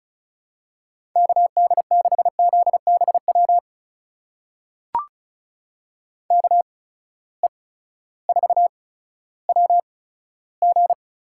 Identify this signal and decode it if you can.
Morse code, heard as a keyed tone, about 36 wpm, spaced as for 16 wpm, 700 Hz. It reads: KD6ZBW KE4WG
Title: USA Call Signs - 6 Characters - 8x Character Spacing - 35wpm